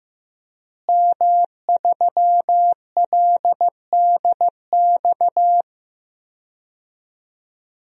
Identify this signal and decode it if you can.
Morse code, heard as a keyed tone, about 15 wpm, 700 Hz. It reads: M3LDX
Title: DXCC-100 Call Signs - 5 Characters - Repeated 2x - 15wpm